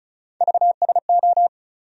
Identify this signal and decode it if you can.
Morse code, heard as a keyed tone, about 36 wpm, 700 Hz. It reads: VSO
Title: USA Call Sign Postfixes - 3 Letters 35wpm